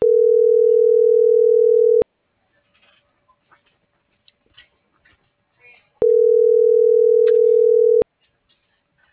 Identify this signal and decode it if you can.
Ringback tone (North America), 440 + 480 Hz.